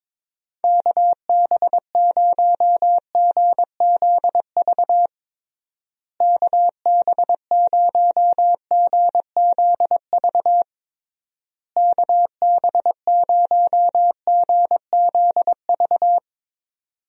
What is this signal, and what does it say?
Morse code, heard as a keyed tone, about 21 wpm, 700 Hz. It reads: KB0GZV KB0GZV KB0GZV
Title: USA Call Signs - 6 Characters - Repeated 3x - 22wpm